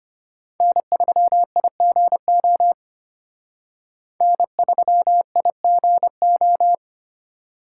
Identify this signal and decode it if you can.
Morse code, heard as a keyed tone, about 32 wpm, 700 Hz. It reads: N3IGO N3IGO